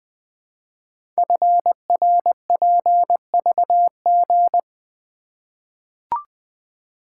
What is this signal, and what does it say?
Morse code, heard as a keyed tone, about 20 wpm, 700 Hz. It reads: FRPVG
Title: Code Groups - 5 Letters - 20wpm